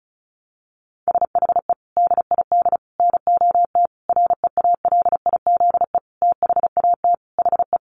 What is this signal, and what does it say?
Morse code, heard as a keyed tone, about 34 wpm, 700 Hz. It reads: SHE DID NOT REALIZE THAT HE
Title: Sentences from Top 1000 Words 35wpm